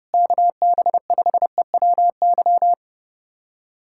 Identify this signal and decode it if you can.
Morse code, heard as a keyed tone, about 28 wpm, 700 Hz. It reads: KB5EWY
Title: USA Callsign DE Callsign - 6 Characters - 30wpm